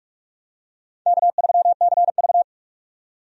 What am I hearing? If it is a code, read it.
Morse code, heard as a keyed tone, about 46 wpm, 700 Hz. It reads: K3CV
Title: USA Call Signs - 4 Characters - 45wpm